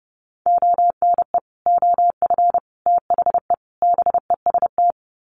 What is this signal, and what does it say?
Morse code, heard as a keyed tone, about 28 wpm, 700 Hz. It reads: ONE OF THE BEST
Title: Sets of 4 Words 30wpm